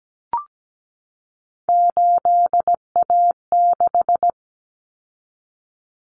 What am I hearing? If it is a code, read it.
Morse code, heard as a keyed tone, about 17 wpm, 700 Hz. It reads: 8A6
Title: Common Cut Numbers - 3-Digit 17wpm